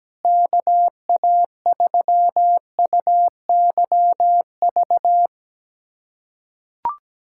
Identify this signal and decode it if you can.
Morse code, heard as a keyed tone, about 17 wpm, 700 Hz. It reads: KA3UYV
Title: USA Callsign DE Callsign - 6 Characters - 17wpm